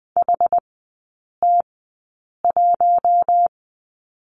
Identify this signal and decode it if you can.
Morse code, heard as a keyed tone, about 20 wpm, spaced as for 8 wpm, 700 Hz. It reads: HT1
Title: 060 Lesson 11e - Letter h - Three Characters 20wpm